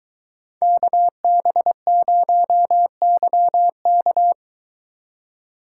Morse code, heard as a keyed tone, about 21 wpm, 700 Hz. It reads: KB0YK